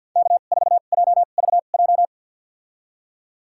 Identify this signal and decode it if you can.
Morse code, heard as a keyed tone, about 52 wpm, spaced as for 36 wpm, 700 Hz. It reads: K4JVJ
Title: USA Call Signs - 5 Characters - 2x Character Spacing - 50wpm